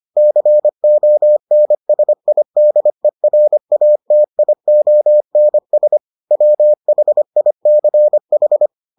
Morse code, heard as a keyed tone, about 25 wpm, 600 Hz. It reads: CONSIDERATIONS WHICH